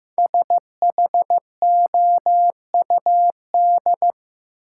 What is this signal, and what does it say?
Morse code, heard as a keyed tone, about 15 wpm, 700 Hz. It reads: SHOUD
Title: Sentences from Top 1000 Words - Review Part 4 - New Word Drill - 15wpm